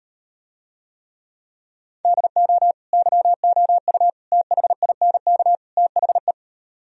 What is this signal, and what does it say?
Morse code, heard as a keyed tone, about 38 wpm, 700 Hz. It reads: DO YOU THINK THE